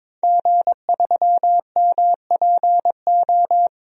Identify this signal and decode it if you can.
Morse code, heard as a keyed tone, about 22 wpm, 700 Hz. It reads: G3MPO
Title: DXCC-100 Call Signs - 5 Characters - 22wpm